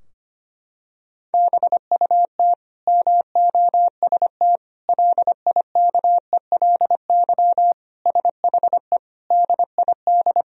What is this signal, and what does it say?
Morse code, heard as a keyed tone, about 25 wpm, 700 Hz. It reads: BUT MOST LIKELY SHE DID
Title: Sentences from Top 1000 Words - Review Part 4 - 25wpm